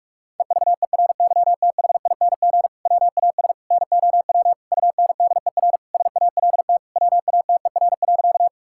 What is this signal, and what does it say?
Morse code, heard as a keyed tone, about 46 wpm, 700 Hz. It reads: EVERYTHING WAS NOW UNDER SALT WATER.
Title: Sentences from Top 1000 Words - Review Part 5 - 45wpm